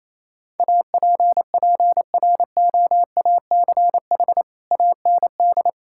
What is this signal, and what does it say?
Morse code, heard as a keyed tone, about 28 wpm, 700 Hz. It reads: APPROACH AND
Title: Sentences from Top 1600 Words 28wpm